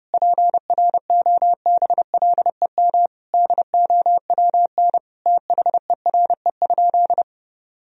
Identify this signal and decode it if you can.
Morse code, heard as a keyed tone, about 30 wpm, 700 Hz. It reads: PROBLEM DOWN THERE?
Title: Sentences from Top 1000 Words - Review Part 4 - 30wpm